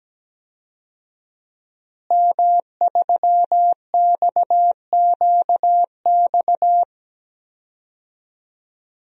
Morse code, heard as a keyed tone, about 17 wpm, 700 Hz. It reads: M3XQX
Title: DXCC-100 Call Signs - 5 Characters - Repeated 3x - 17wpm